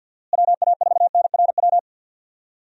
Morse code, heard as a keyed tone, about 52 wpm, 700 Hz. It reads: WA4NRW
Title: USA Call Signs - 6 Characters - Repeated 3x - 50wpm